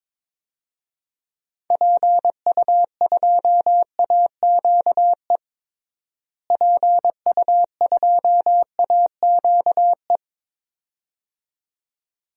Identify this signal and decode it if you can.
Morse code, heard as a keyed tone, about 22 wpm, 700 Hz. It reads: PU2AQE PU2AQE
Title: DXCC-100 Call Signs - 6 Characters - Repeated 2x - 22wpm